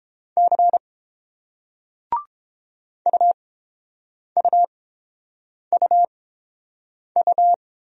Morse code, heard as a keyed tone, about 30 wpm, spaced as for 7 wpm, 700 Hz. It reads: C UUUU